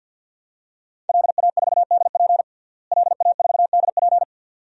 Morse code, heard as a keyed tone, about 53 wpm, 700 Hz. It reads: LA4DP LA4DP
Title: DXCC-100 Call Signs - 5 Characters - Repeated 2x - 50wpm